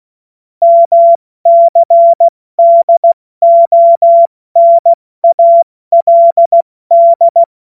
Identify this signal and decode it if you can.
Morse code, heard as a keyed tone, about 16 wpm, 680 Hz. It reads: MCDONALD
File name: Morsekurs / Witze / Witz 06. April 2021, 70 F80